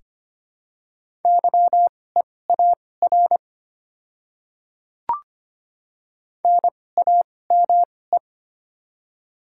Morse code, heard as a keyed tone, about 25 wpm, spaced as for 18 wpm, 700 Hz. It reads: YEAR NAME
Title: 2x Character Spacing - Top 200 Words - 25wpm